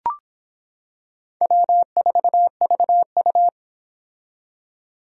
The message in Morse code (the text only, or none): W4VU